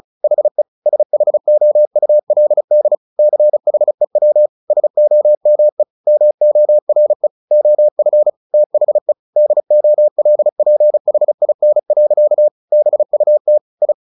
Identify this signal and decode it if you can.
Morse code, heard as a keyed tone, about 34 wpm, 600 Hz. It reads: HE SHOULD CHEW SOME MORE OF THE DOLPHIN. BUT I
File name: the old man and the sea 2 35wpm